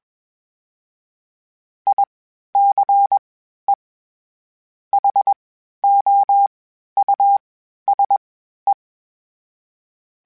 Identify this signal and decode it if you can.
Morse code, heard as a keyed tone, about 21 wpm, 800 Hz. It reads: I C E H O U S E
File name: General / General-mp3s / Houses @20